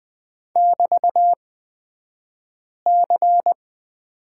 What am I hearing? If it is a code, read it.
Morse code, heard as a keyed tone, about 20 wpm, spaced as for 5 wpm, 700 Hz. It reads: <BT>C
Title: 242 Lesson 39d - Letter z - Two Characters 20wpm